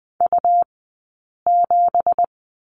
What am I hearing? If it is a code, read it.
Morse code, heard as a keyed tone, about 20 wpm, spaced as for 8 wpm, 700 Hz. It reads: U7
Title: 165 Lesson 27d - Number 7 - Two Characters 20wpm